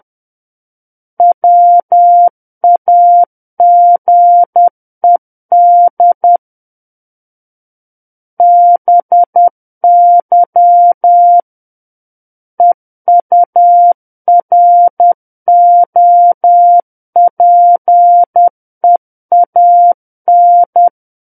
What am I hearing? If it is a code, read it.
Morse code, heard as a keyed tone, about 10 wpm, 700 Hz. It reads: WAGED BY EUROPEAN
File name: Book One / 01-Chapter 1 - 10 WPM